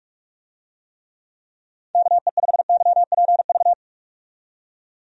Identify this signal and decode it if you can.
Morse code, heard as a keyed tone, about 46 wpm, 700 Hz. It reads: KE5YPV
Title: USA Call Signs - 6 Characters - 45wpm